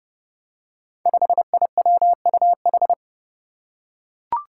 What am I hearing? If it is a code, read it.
Morse code, heard as a keyed tone, about 30 wpm, 700 Hz. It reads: 5IWUH
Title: Five Challenging Letter-Number Combinations 30wpm